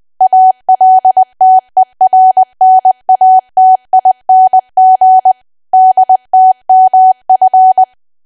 Morse code, heard as a keyed tone, about 20 wpm, 750 Hz. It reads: ALTERNATING DTMF
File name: Archive / 20 WPM / 20WPM